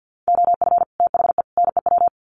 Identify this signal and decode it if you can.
Morse code, heard as a keyed tone, about 50 wpm, 700 Hz. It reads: OF THE NEW
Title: Sets of 3 Words 50wpm